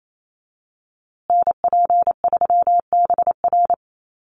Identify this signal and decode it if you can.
Morse code, heard as a keyed tone, about 28 wpm, 700 Hz. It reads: NP3BR
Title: USA Call Signs Repeated 2x 28wpm